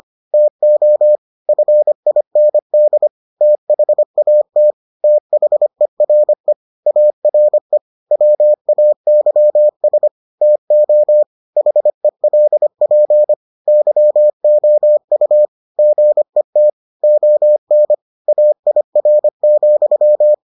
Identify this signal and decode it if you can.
Morse code, heard as a keyed tone, about 25 wpm, 600 Hz. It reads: TO FIND THAT THERE ARE WAYS TO HELP YOU GET ON AIR,